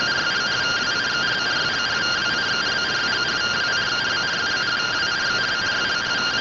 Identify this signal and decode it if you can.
Two-tone teleprinter signal, about 170 Hz shift.